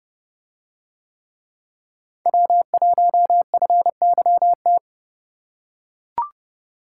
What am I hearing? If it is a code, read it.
Morse code, heard as a keyed tone, about 29 wpm, 700 Hz. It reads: W1FYT